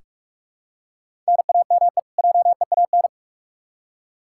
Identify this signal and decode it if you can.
Morse code, heard as a keyed tone, about 46 wpm, 700 Hz. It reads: NAME JEAN
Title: Practice QSO - Exchange 1 - Beginning - 45wpm